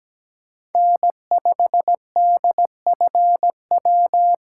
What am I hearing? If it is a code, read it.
Morse code, heard as a keyed tone, about 17 wpm, 700 Hz. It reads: N5DFW
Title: USA Call Signs - 3x Character Spacing - 17wpm